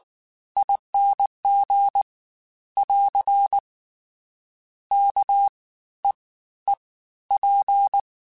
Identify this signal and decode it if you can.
Morse code, heard as a keyed tone, about 19 wpm, spaced as for 13 wpm, 800 Hz. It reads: ING <AR> KEEP
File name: Special INGs-mp3s / Special INGs-mp3s / Fast ing @18